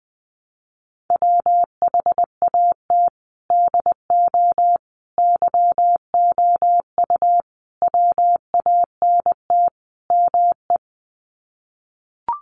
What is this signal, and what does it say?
Morse code, heard as a keyed tone, about 20 wpm, 700 Hz. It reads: WHAT DO YOU WANT ME
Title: Sets of 5 Words Encore2 20wpm